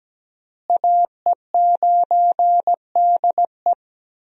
Morse code, heard as a keyed tone, about 17 wpm, 700 Hz. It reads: AE9DE
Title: USA Call Signs - 5 Characters - Repeated 2x - 17wpm